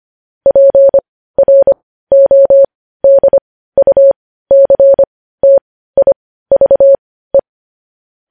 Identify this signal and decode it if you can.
Morse code, heard as a keyed tone, about 25 wpm, spaced as for 15 wpm, 550 Hz. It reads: PRODUCTIVE
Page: news-headlines-in-morse-code-at-15-wpm